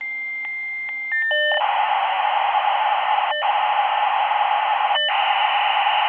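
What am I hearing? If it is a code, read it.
Dial-up modem handshake.
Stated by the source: Computer modem working sound effect from Sound Effects Library.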